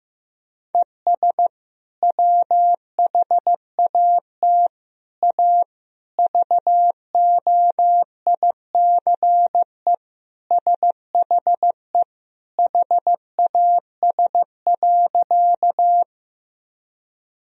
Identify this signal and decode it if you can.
Morse code, heard as a keyed tone, about 16 wpm, 700 Hz. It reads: ES WHAT A VOICE SHE HAS.
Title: Sentences from Top 500 Words and Abbreviations 15wpm